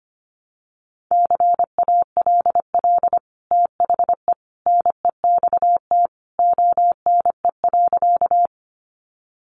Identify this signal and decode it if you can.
Morse code, heard as a keyed tone, about 25 wpm, 700 Hz. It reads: CALL THE NEXT ONE.
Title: Sentences from Top 300 Words 25wpm